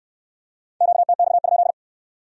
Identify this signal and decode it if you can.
Morse code, heard as a keyed tone, about 65 wpm, 700 Hz. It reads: QTX?